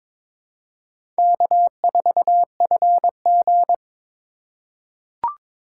Morse code, heard as a keyed tone, about 22 wpm, 700 Hz. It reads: K4FG
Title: USA Call Signs - 4 Characters - 2x Character Spacing - 22wpm